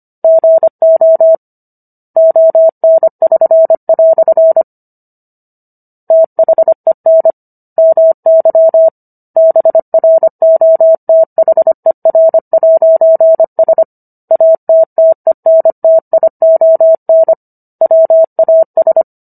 Morse code, heard as a keyed tone, about 25 wpm, 650 Hz. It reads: GO ON<SN>" THEN MY BROTHER'S ATTENTION WAS